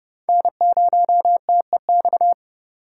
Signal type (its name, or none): Morse code